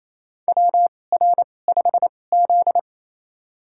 Morse code, heard as a keyed tone, about 28 wpm, spaced as for 20 wpm, 700 Hz. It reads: WR5Z
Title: USA Call Signs - 2x Character Spacing - 28wpm